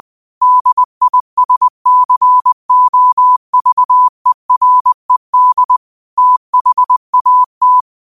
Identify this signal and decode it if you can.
Morse code, heard as a keyed tone, about 20 wpm, 1000 Hz. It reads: DISCOVERED THAT
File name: qotdfiles / 20wpm